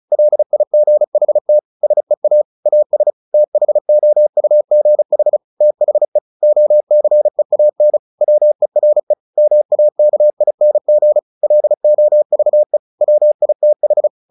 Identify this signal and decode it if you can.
Morse code, heard as a keyed tone, about 34 wpm, 600 Hz. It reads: LIGHT SEA AS THOUGH THE OCEAN WERE MAKING LOVE WITH
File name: the old man and the sea 2 35wpm